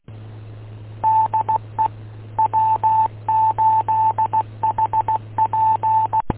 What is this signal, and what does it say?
Morse code, heard as a keyed tone, about 16 wpm, 880 Hz, with radio noise. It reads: DE W8HP
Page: w8um-145230